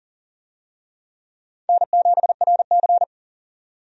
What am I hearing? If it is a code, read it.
Morse code, heard as a keyed tone, about 39 wpm, 700 Hz. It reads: N7RC
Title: USA Call Signs - 4 Characters - 40wpm